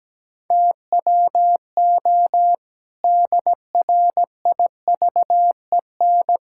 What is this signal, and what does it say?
Morse code, heard as a keyed tone, about 17 wpm, 700 Hz. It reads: TWO DRIVEN
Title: Pangram Challenge v2 - 17wpm